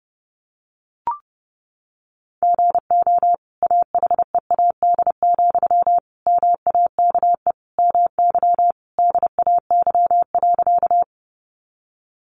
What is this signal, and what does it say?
Morse code, heard as a keyed tone, about 30 wpm, 700 Hz. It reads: GO AHEAD, MAKE MY DAY.
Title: Sentences from Top 1100 Words 30wpm